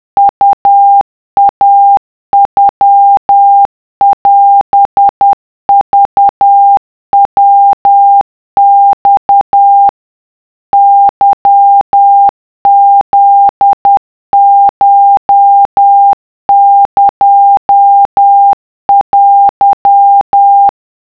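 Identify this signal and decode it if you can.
Morse code (Japanese Wabun code), heard as a keyed tone, about 10 wpm, 800 Hz. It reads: ウイノオクヤマ ケフコエテ
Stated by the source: Japanese wabun Morse of the iroha poem, generated as a wav by a text-to-Morse app at 50 CPM (10 WPM).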